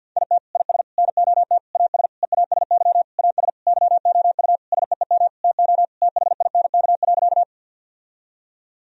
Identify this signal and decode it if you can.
Morse code, heard as a keyed tone, about 51 wpm, 700 Hz. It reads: IT IS NOT AS EASY AS YOU SEEM TO THINK.
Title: Sentences from Top 1000 Words - Review Part 1 - 50wpm